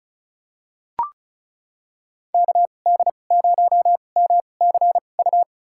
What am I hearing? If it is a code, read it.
Morse code, heard as a keyed tone, about 36 wpm, spaced as for 25 wpm, 700 Hz. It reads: KD0MCU
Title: USA Call Signs - 2x Character Spacing - 35wpm